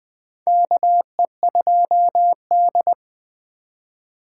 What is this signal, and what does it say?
Morse code, heard as a keyed tone, about 20 wpm, 700 Hz. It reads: KE2D
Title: K1USN SST QSO Practice - 20wpm